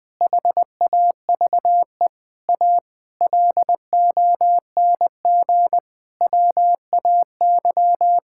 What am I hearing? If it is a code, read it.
Morse code, heard as a keyed tone, about 20 wpm, 700 Hz. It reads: HAVE A LONG WAY
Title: Sentences with 9-Word Phrases - 20wpm